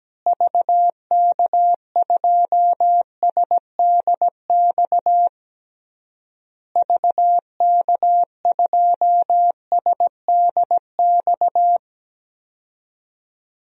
Morse code, heard as a keyed tone, about 17 wpm, 700 Hz. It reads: VK2SDX VK2SDX
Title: DXCC-100 Call Signs - 6 Characters - Repeated 3x - 17wpm